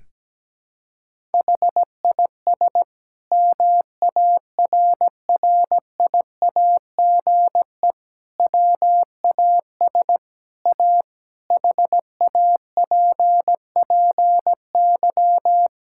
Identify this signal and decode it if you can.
Morse code, heard as a keyed tone, about 17 wpm, 700 Hz. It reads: HIS MARRIAGE WAS A HAPPY
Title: Sentences from Top 1000 Words - Review Part 5 - 17wpm